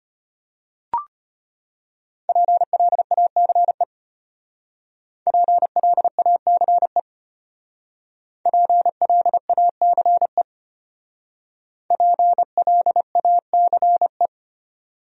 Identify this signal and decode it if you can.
Morse code, heard as a keyed tone, about 31 wpm, 700 Hz. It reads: PLACE PLACE PLACE PLACE